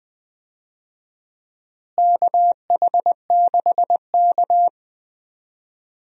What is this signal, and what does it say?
Morse code, heard as a keyed tone, about 20 wpm, 700 Hz. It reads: KH6K
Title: DXCC-100 Call Signs - 4 Characters - Repeated 2x - 20wpm